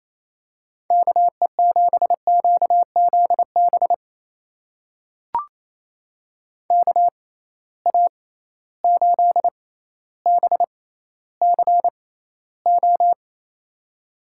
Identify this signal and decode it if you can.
Morse code, heard as a keyed tone, about 27 wpm, spaced as for 14 wpm, 700 Hz. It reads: KE7QZB KA8BCO
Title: USA Call Signs - 6 Characters - 6x Character Spacing - 28wpm